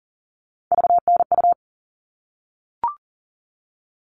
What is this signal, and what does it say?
Morse code, heard as a keyed tone, about 40 wpm, 700 Hz. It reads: VNU
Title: All Cut Numbers - 3-Digit 40wpm